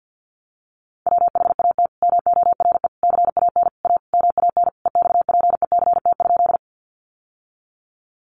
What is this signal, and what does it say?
Morse code, heard as a keyed tone, about 50 wpm, 700 Hz. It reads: WHAT MORE CAN A MAN EXPECT?
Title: Sentences from Top 1100 Words 50wpm